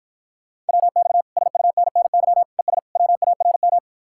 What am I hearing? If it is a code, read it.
Morse code, heard as a keyed tone, about 52 wpm, 700 Hz. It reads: WX SUNNY ES WARM